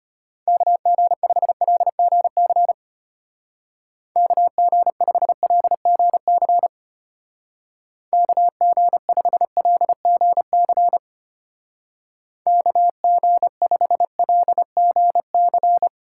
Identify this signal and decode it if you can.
Morse code, heard as a keyed tone, about 32 wpm, 700 Hz. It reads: KG5LGC KG5LGC KG5LGC KG5LGC